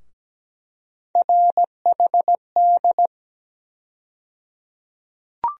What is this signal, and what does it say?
Morse code, heard as a keyed tone, about 17 wpm, 700 Hz. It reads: RHD
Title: USA Call Sign Postfixes - 3 Letters 17wpm